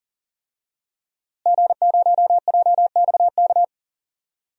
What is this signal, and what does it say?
Morse code, heard as a keyed tone, about 40 wpm, 700 Hz. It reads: G0JXK
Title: DXCC-100 Call Signs - 5 Characters - Repeated 2x - 40wpm